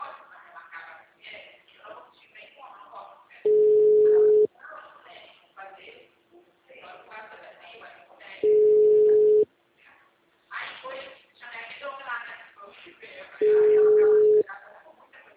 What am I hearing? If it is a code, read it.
Ringback tone (Europe, 425 Hz), with speech.
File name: out- -2005- - - .50742